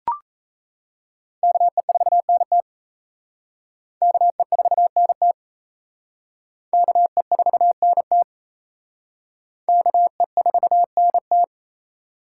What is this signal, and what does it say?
Morse code, heard as a keyed tone, about 34 wpm, 700 Hz. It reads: KE4NT KE4NT KE4NT KE4NT
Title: Speed Racing - USA 2x2 Call Signs 28wpm